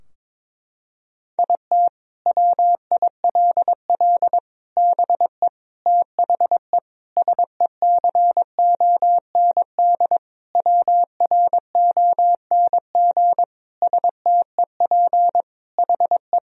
Morse code, heard as a keyed tone, about 22 wpm, 700 Hz. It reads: IT WILL BE THE SECOND WRONG STEP HE
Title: Sentences from Top 1000 Words - Review Part 2 - 22wpm